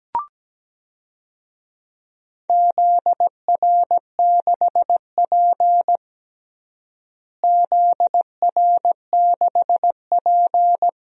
Morse code, heard as a keyed tone, about 17 wpm, 700 Hz. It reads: ZR6P ZR6P
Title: DXCC-100 Call Signs - 4 Characters - Repeated 3x - 17wpm